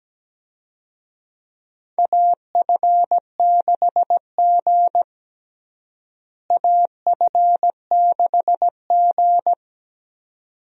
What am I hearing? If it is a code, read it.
Morse code, heard as a keyed tone, about 17 wpm, 700 Hz. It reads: AF6G AF6G
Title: USA Call Signs - 4 Characters - Repeated 2x - 17wpm